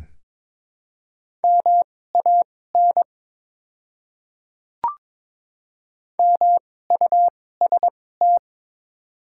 Morse code, heard as a keyed tone, about 22 wpm, spaced as for 16 wpm, 700 Hz. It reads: MAN MUST
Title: 2x Character Spacing - Top 200 Words - 22wpm